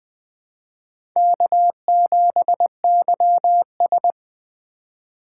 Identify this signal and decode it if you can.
Morse code, heard as a keyed tone, about 20 wpm, 700 Hz. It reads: K7YS
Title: USA Call Signs - 4 Characters - 4x Character Spacing - 20wpm